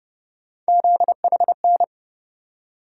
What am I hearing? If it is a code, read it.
Morse code, heard as a keyed tone, about 31 wpm, 700 Hz. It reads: ZHN
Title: USA Call Sign Postfixes - 3 Letters 30wpm